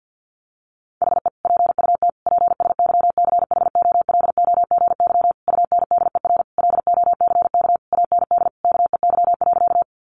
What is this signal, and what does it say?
Morse code, heard as a keyed tone, about 50 wpm, 700 Hz. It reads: HE PUT PSYCHOLOGY UNDER LOCK AND KEY.
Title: Sentences from Top 1400 Words 50wpm